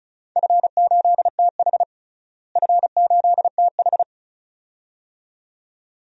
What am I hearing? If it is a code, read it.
Morse code, heard as a keyed tone, about 36 wpm, 700 Hz. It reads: F8TH F8TH
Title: DXCC-100 Call Signs - 4 Characters - Repeated 2x - 35wpm